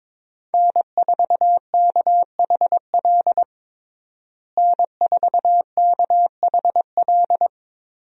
Morse code, heard as a keyed tone, about 22 wpm, 700 Hz. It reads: N4KHL N4KHL
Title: USA Call Signs - 5 Characters - Repeated 2x - 22wpm